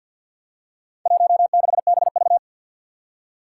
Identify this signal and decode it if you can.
Morse code, heard as a keyed tone, about 49 wpm, 700 Hz. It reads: 16BV
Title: Four Challenging Letter-Number Combinations 50wpm